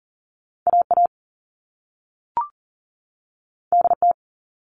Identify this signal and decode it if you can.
Morse code, heard as a keyed tone, about 40 wpm, 700 Hz. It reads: AA DT
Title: All Cut Numbers - 2-Digit 40wpm